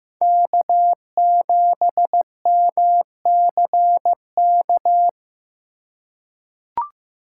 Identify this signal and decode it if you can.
Morse code, heard as a keyed tone, about 16 wpm, 700 Hz. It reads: K7MCK
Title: USA Call Signs - 5 Characters - 3x Character Spacing - 15wpm